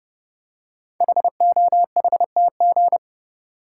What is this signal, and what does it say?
Morse code, heard as a keyed tone, about 28 wpm, 700 Hz. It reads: HOHTG
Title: Code Groups - 5 Letters - 30wpm